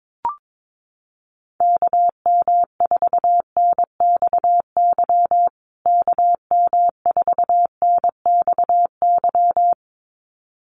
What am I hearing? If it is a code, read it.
Morse code, heard as a keyed tone, about 22 wpm, 700 Hz. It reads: KM4NXY KM4NXY
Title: USA Call Signs Repeated 2x 22wpm